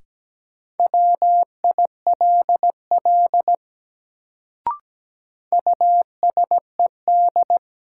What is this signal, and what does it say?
Morse code, heard as a keyed tone, about 17 wpm, 700 Hz. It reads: WILL USED
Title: Sets of 2 Words - New Word Drill - 17wpm